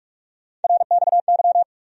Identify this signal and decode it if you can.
Morse code, heard as a keyed tone, about 46 wpm, 700 Hz. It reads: RXY
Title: USA Call Sign Postfixes - 3 Letters 45wpm